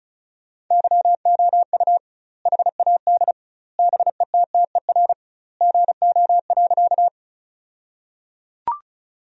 Morse code, heard as a keyed tone, about 36 wpm, 700 Hz. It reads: YOU HAD BETTER GO.
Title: 2x Word-Spacing - Sentences from Top 400 Words - 35wpm